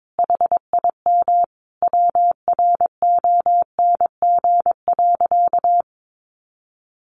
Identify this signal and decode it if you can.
Morse code, heard as a keyed tone, about 22 wpm, 700 Hz. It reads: HIM WRONG.
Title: Sentences from Top 1300 Words 22wpm